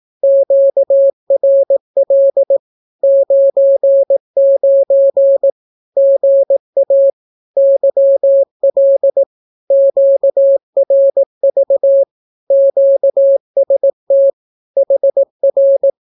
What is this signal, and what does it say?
Morse code, heard as a keyed tone, about 18 wpm, 550 Hz. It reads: QRL 99 GA YL QRV QST HR
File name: naukapodstaw / skroty / skroty